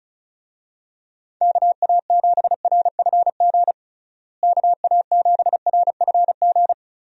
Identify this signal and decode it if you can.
Morse code, heard as a keyed tone, about 36 wpm, 700 Hz. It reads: KA7RFG KA7RFG
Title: USA Call Signs - 6 Characters - Repeated 2x - 35wpm